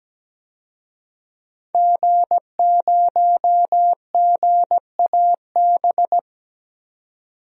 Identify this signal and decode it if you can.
Morse code, heard as a keyed tone, about 17 wpm, 700 Hz. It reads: G0GAB
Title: DXCC-100 Call Signs - 5 Characters - 17wpm